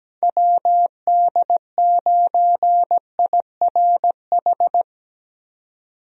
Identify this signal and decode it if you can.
Morse code, heard as a keyed tone, about 17 wpm, 700 Hz. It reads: WD9IRH
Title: USA Call Signs - 6 Characters - 17wpm